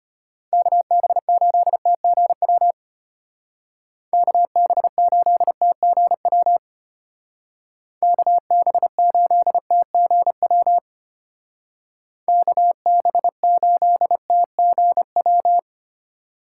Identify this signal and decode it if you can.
Morse code, heard as a keyed tone, about 30 wpm, 700 Hz. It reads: KB8TGW KB8TGW KB8TGW KB8TGW
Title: Speed Racing - USA 2x3 Call Signs 25wpm